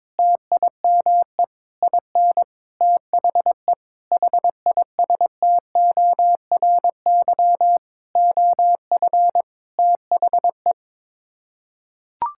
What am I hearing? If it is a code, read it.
Morse code, heard as a keyed tone, about 22 wpm, 700 Hz. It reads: TIME IN THE HISTORY OF THE
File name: Sets of 9 Words - 22wpm